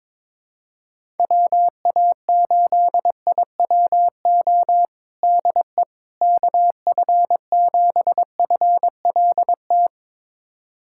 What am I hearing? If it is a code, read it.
Morse code, heard as a keyed tone, about 21 wpm, 700 Hz. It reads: WA8IWO DE KF7FLT
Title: USA Callsign DE Callsign 22wpm